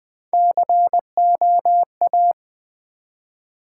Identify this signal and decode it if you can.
Morse code, heard as a keyed tone, about 20 wpm, 700 Hz. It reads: COA
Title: Build A Word - 1 Letter At A Time - 3 to 5-Letter Words - 20wpm